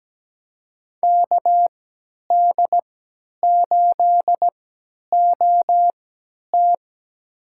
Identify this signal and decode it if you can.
Morse code, heard as a keyed tone, about 17 wpm, spaced as for 10 wpm, 700 Hz. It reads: KD8OT
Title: USA Call Signs - 5 Characters - 3x Character Spacing - 17wpm